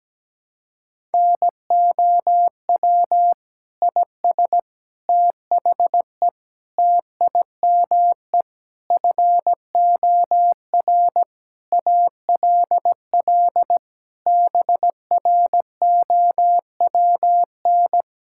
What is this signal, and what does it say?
Morse code, heard as a keyed tone, about 17 wpm, 700 Hz. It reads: NOW IS THE TIME FOR ALL BROWN
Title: Pangram Challenge v2 - 17wpm